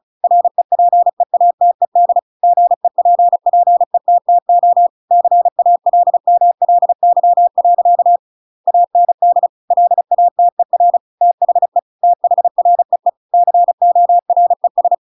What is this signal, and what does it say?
Morse code, heard as a keyed tone, about 36 wpm, 700 Hz. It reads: REPEATED GEPPETTO CALMLY. AND LATER THE THREE CORES
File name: Pinocchio / Pinocchio ch7 35 35 ik1hge